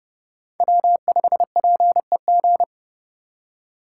Morse code, heard as a keyed tone, about 31 wpm, 700 Hz. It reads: W5PEG
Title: USA Call Signs - 3x Character Spacing - 30wpm